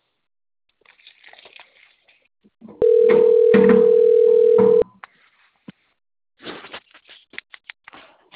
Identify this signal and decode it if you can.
Ringback tone (North America), 440 + 480 Hz.
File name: rg-1600-unknown- - - .67791